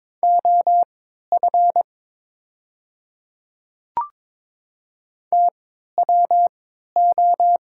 Morse code, heard as a keyed tone, about 23 wpm, spaced as for 13 wpm, 700 Hz. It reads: OF TWO
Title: 3x Character Spacing - Top 100 Words - 22wpm